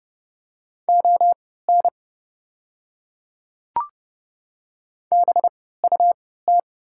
Morse code, heard as a keyed tone, about 30 wpm, spaced as for 17 wpm, 690 Hz. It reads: ON BUT
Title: 3x Character Spacing - Top 100 Words - 30wpm